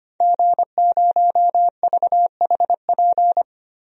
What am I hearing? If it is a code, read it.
Morse code, heard as a keyed tone, about 25 wpm, 700 Hz. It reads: G0VHP
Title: DXCC-100 Call Signs - 5 Characters - 25wpm